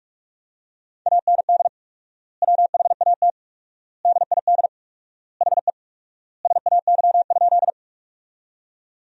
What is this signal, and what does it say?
Morse code, heard as a keyed tone, about 41 wpm, 700 Hz. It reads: AND WHAT DID HE SAY?
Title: 4x Word-Spacing - Sentences from Top 200 Words - 45wpm